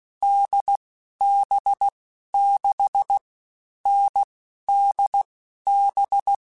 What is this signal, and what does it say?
Morse code, heard as a keyed tone, about 16 wpm, 800 Hz. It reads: D B 6 N D B
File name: Difficult Combos / Difficult Combos-mp3s / NDB NDB6 @15